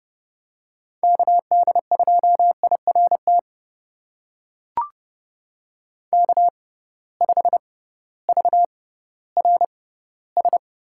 Morse code, heard as a keyed tone, about 30 wpm, spaced as for 16 wpm, 700 Hz. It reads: KD2IRT K5VRS
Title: USA Call Signs - 6x Character Spacing - 30wpm